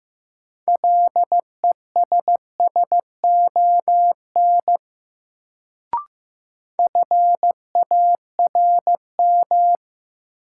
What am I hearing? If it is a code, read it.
Morse code, heard as a keyed tone, about 15 wpm, 700 Hz. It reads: LESSON FARM
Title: Sentences from Top 1200 Words - New Word Drill Alt - 15wpm